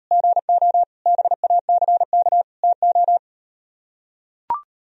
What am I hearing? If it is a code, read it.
Morse code, heard as a keyed tone, about 38 wpm, 700 Hz. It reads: GO BACK TO